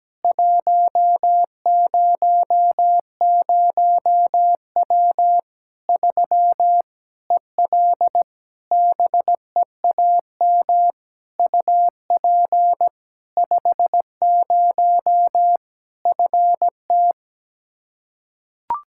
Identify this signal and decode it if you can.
Morse code, heard as a keyed tone, about 17 wpm, 700 Hz. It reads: 100W 3 EL BEAM UP 50 FT
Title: K1USN SST QSO Practice 17wpm